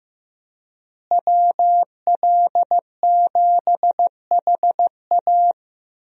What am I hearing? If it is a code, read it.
Morse code, heard as a keyed tone, about 15 wpm, 700 Hz. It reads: WL7HA
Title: USA Call Signs - 5 Characters - Repeated 2x - 15wpm